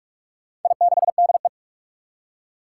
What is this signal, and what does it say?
Morse code, heard as a keyed tone, about 43 wpm, 700 Hz. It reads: I6DE
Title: Four Challenging Letter-Number Combinations 45wpm